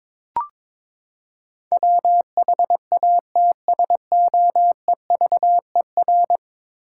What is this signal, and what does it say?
Morse code, heard as a keyed tone, about 22 wpm, 700 Hz. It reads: WHATSOEVER
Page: morse-code-ninja-practice-22wpm